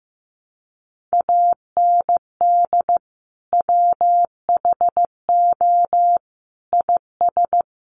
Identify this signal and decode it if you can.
Morse code, heard as a keyed tone, about 15 wpm, 700 Hz. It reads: AND WHO IS
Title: Sentences from Top 100 Words 15wpm